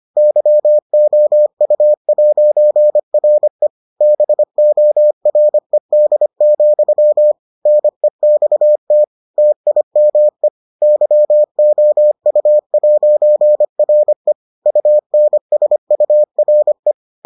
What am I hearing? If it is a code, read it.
Morse code, heard as a keyed tone, about 25 wpm, 600 Hz. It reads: YOU'RE BORED, NEXT TIME YOU'RE UNSURE